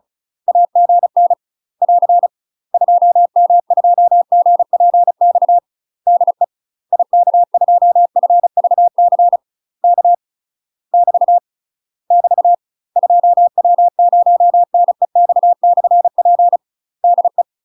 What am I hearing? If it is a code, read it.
Morse code, heard as a keyed tone, about 35 wpm, 700 Hz. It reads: AGN <AR> 2M2GPX DE IK2FVC K <BT> <BT> 2W0NEX/P DE
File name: UK MASTER QSOs TRAINING / 35WPM / UK MASTER QSOs 0032